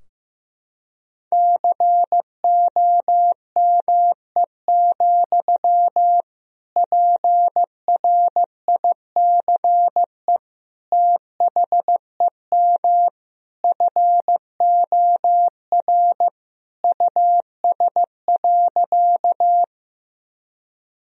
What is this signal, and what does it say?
Morse code, heard as a keyed tone, about 15 wpm, 700 Hz. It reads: COME, PRICE THEM FOR US.
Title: Sentences from Top 1000 Words - Review Part 4 - 15wpm